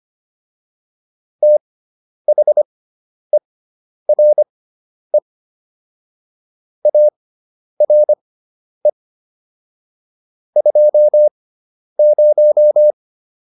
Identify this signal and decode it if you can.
Morse code, heard as a keyed tone, about 25 wpm, spaced as for 10 wpm, 600 Hz. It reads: THERE ARE 20